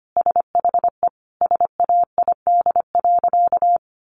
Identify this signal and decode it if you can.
Morse code, heard as a keyed tone, about 25 wpm, 700 Hz. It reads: SHE SAID.
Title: Sentences from Top 1500 Words 25wpm